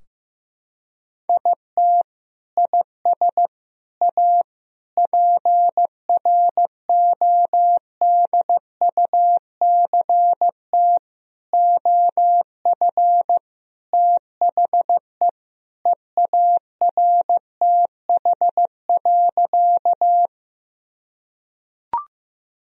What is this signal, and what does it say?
Morse code, heard as a keyed tone, about 15 wpm, 700 Hz. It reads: IT IS A PRODUCT OF THE EARTH.
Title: Sentences from Top 1000 Words - Review Part 2 - 15wpm